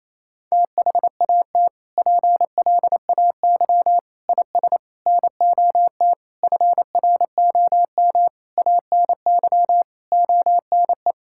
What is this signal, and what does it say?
Morse code, heard as a keyed tone, about 28 wpm, 700 Hz. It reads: THAT PLAY IS NOT FROM ANY ONE
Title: Sentences from Top 1000 Words - Review Part 5 - 28wpm